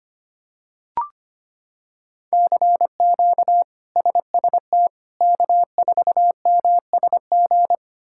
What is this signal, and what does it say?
Morse code, heard as a keyed tone, about 25 wpm, 700 Hz. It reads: CQ SST K4MSG
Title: K1USN SST QSO Practice - 25wpm